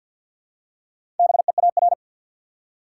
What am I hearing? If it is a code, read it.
Morse code, heard as a keyed tone, about 51 wpm, 700 Hz. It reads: BEAR
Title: Four Letter Words - 50wpm